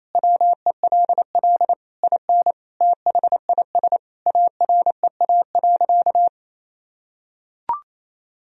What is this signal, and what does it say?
Morse code, heard as a keyed tone, about 28 wpm, 700 Hz. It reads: WELL IN THIS AREA.
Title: Sentences from Top 1000 Words - Review Part 2 - 28wpm